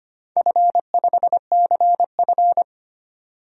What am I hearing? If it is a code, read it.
Morse code, heard as a keyed tone, about 25 wpm, 700 Hz. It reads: F5CF
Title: DXCC-100 Call Signs - 4 Characters - 25wpm